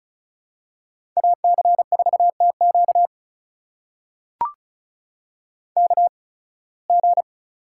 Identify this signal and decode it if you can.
Morse code, heard as a keyed tone, about 36 wpm, spaced as for 23 wpm, 700 Hz. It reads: AC4TQ KG
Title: USA Call Signs - 5 Characters - 8x Character Spacing - 35wpm